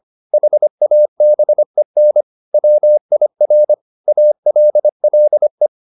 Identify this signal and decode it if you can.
Morse code, heard as a keyed tone, about 25 wpm, 600 Hz. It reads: HABEN WIR ALLE